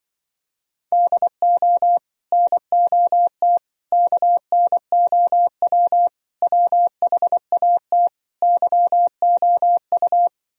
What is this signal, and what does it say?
Morse code, heard as a keyed tone, about 24 wpm, 700 Hz. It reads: DO NOT KNOW WHAT YOU
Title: Speed-Racing - Sets of 5 Words 20wpm